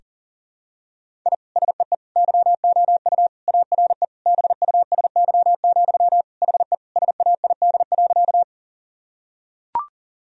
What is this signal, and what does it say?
Morse code, heard as a keyed tone, about 40 wpm, 700 Hz. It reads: I SEE YOU ARE BUSY, HE SAID.
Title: Sentences from Top 1000 Words - Review Part 3 - 40wpm